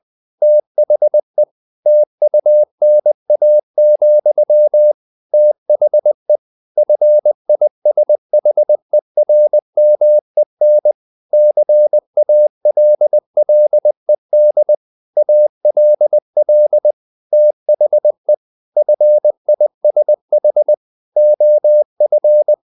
Morse code, heard as a keyed tone, about 20 wpm, 600 Hz. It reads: THE TUNA, THE FISHERMEN CALLED ALL THE FISH OF
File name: the old man and the sea / the old man and the sea part 1 20wpm